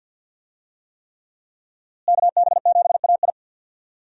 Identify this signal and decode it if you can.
Morse code, heard as a keyed tone, about 48 wpm, 700 Hz. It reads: KB7AI
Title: USA Call Signs - 5 Characters - Repeated 3x - 50wpm